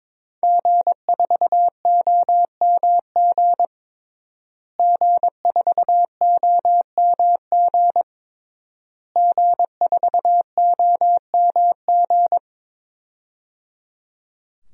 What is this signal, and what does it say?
Morse code, heard as a keyed tone, about 22 wpm, 700 Hz. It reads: G4OMG G4OMG G4OMG
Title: DXCC-100 Call Signs - 5 Characters - Repeated 3x - 22wpm